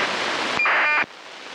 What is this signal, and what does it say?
ACARS aircraft data.